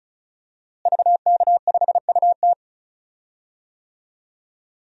Morse code, heard as a keyed tone, about 34 wpm, 700 Hz. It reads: VK5UT
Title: DXCC-100 Call Signs - 5 Characters - Repeated 2x - 35wpm